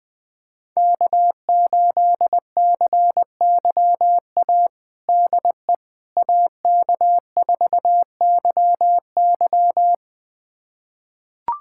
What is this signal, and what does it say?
Morse code, heard as a keyed tone, about 20 wpm, 700 Hz. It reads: K8CYA DE AK4YY
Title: USA Callsign DE Callsign - 5 Characters - 20wpm